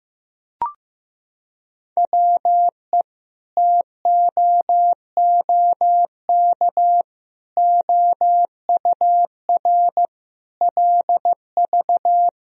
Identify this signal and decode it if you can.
Morse code, heard as a keyed tone, about 16 wpm, 700 Hz. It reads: WE TOOK OUR LV
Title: Sentences from Top 400 Words and Abbreviations 15wpm